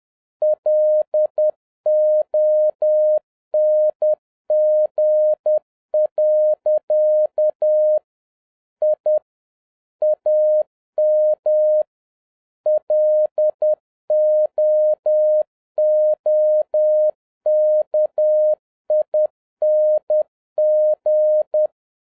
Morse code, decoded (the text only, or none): LONG. I AM LOOKING